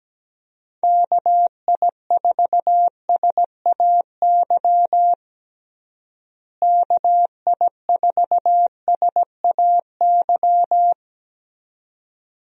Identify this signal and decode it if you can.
Morse code, heard as a keyed tone, about 17 wpm, 700 Hz. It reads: KI4SAY KI4SAY
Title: USA Call Signs - 6 Characters - Repeated 2x - 17wpm